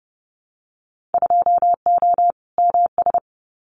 Morse code, heard as a keyed tone, about 30 wpm, 700 Hz. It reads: 2O MS
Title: Winter Field Day Exchanges 30wpm